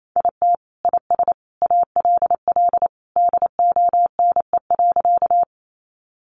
Morse code, heard as a keyed tone, about 28 wpm, 700 Hz. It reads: IT IS ALL DONE.
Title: Sentences from Top 500 Words 28wpm